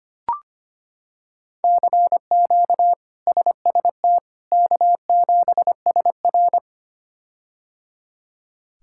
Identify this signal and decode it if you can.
Morse code, heard as a keyed tone, about 25 wpm, 700 Hz. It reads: CQ SST K7SR
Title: K1USN SST QSO Practice - 25wpm